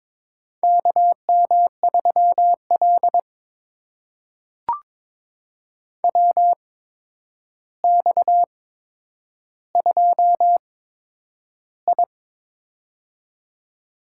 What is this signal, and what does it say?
Morse code, heard as a keyed tone, about 23 wpm, spaced as for 10 wpm, 700 Hz. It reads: KM3L WX2I
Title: USA Call Signs - 4 Characters - 8x Character Spacing - 22wpm